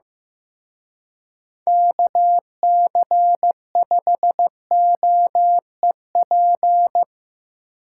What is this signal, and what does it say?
Morse code, heard as a keyed tone, about 17 wpm, 700 Hz. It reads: KC5OEP
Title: Speed Racing - USA 2x3 Call Signs 15wpm